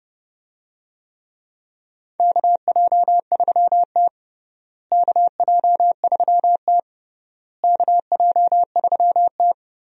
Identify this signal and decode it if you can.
Morse code, heard as a keyed tone, about 32 wpm, 700 Hz. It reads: KJ3T KJ3T KJ3T
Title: USA Call Signs - 4 Characters - Repeated 3x - 30wpm